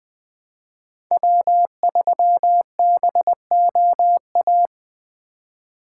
Morse code, heard as a keyed tone, about 21 wpm, 700 Hz. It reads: W3BOA